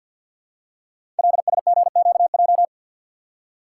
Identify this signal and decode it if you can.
Morse code, heard as a keyed tone, about 52 wpm, 700 Hz. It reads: LSGQJ
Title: Code Groups - 5 Letter:Numbers - 50wpm